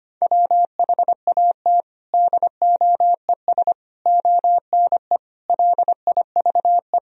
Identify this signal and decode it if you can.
Morse code, heard as a keyed tone, about 25 wpm, 700 Hz. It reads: WHAT DOES ONE LIVE
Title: Sentences from Top 200 Words and Abbreviations 25wpm